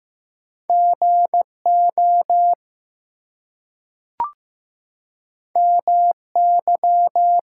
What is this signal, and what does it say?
Morse code, heard as a keyed tone, about 15 wpm, 700 Hz. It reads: GO MY
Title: Two Letter Words - 15wpm